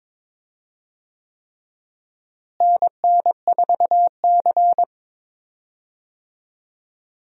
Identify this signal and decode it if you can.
Morse code, heard as a keyed tone, about 22 wpm, 700 Hz. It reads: NN4C
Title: USA Call Signs - 4 Characters - Repeated 3x - 22wpm